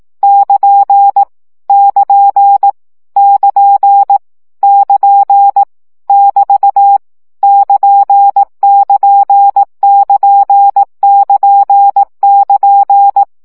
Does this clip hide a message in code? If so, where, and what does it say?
Yes: Morse code from 0.2 to 13.2 s, heard as a keyed tone, about 18 wpm, 800 Hz. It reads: <KN> <KN> <KN> <KN> <BT> <KN><KN><KN><KN><KN>